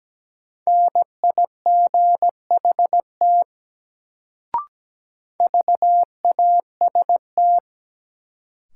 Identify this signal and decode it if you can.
Morse code, heard as a keyed tone, about 17 wpm, 700 Hz. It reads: NIGHT VAST
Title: Sets of 4 Words - New Word Drill - 17wpm